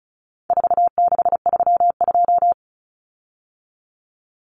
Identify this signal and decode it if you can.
Morse code, heard as a keyed tone, about 34 wpm, 700 Hz. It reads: 4632
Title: Common Cut Numbers - 4-Digit 35wpm